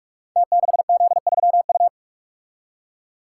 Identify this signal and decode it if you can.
Morse code, heard as a keyed tone, about 44 wpm, 700 Hz. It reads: T6Z3U
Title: Code Groups - 5 Letter:Numbers - 45wpm